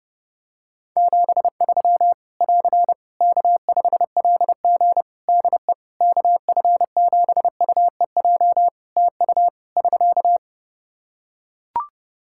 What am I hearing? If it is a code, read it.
Morse code, heard as a keyed tone, about 32 wpm, 700 Hz. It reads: 73 <AR> K5LG DE KF7UEJ TU <SK>
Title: K1USN SST QSO Practice 30wpm